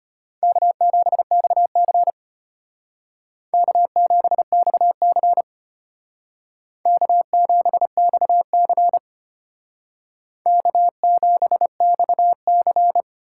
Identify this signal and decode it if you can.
Morse code, heard as a keyed tone, about 32 wpm, 700 Hz. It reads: K7XC K7XC K7XC K7XC